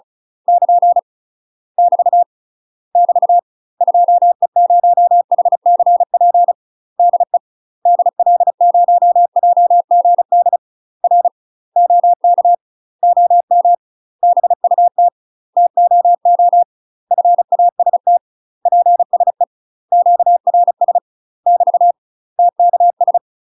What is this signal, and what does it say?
Morse code, heard as a keyed tone, about 34 wpm, 700 Hz. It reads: <KN> <BT> <BT> 2E0HCP DE DL0JGD R OK OM BUT TOO FAST PSE QRS <BT> TKS
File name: UK MASTER QSOs TRAINING / 35WPM / UK MASTER QSOs 0031